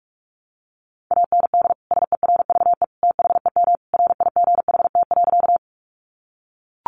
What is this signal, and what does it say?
Morse code, heard as a keyed tone, about 44 wpm, 700 Hz. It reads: AND SERVE THEM RIGHT.
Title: Sentences from Top 400 Words 45wpm